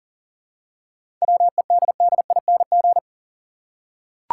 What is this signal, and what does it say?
Morse code, heard as a keyed tone, about 40 wpm, 700 Hz. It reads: WEDDING
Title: Seven Letter Words - 40wpm